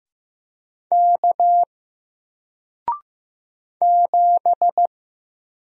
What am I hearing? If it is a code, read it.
Morse code, heard as a keyed tone, about 15 wpm, 700 Hz. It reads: K 7
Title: Single Character (Letter:Number:Punctuation:Prosign) Alt 15wpm